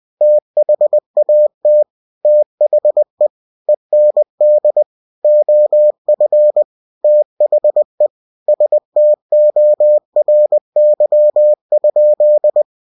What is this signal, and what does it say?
Morse code, heard as a keyed tone, about 20 wpm, 600 Hz. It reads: THAT THE END OF THE STORY?
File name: cw-books / winnie-the-pooh / winnie-20wpm-page-0011